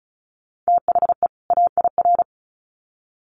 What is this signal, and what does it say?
Morse code, heard as a keyed tone, about 34 wpm, 700 Hz. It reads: THE AIR
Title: Sets of 2 Words 35wpm